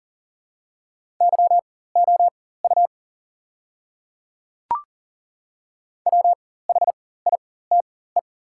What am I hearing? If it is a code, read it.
Morse code, heard as a keyed tone, about 40 wpm, spaced as for 19 wpm, 700 Hz. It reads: YOU WHITE
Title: 4x Character Spacing - Top 300 Words - 40wpm